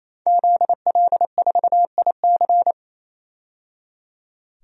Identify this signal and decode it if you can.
Morse code, heard as a keyed tone, about 28 wpm, 700 Hz. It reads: ZL4IC
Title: DXCC-100 Call Signs - 5 Characters - 28wpm